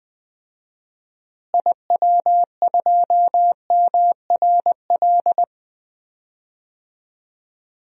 Morse code, heard as a keyed tone, about 20 wpm, 700 Hz. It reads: IW2MRL
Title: DXCC-100 Call Signs - 6 Characters - Repeated 3x - 20wpm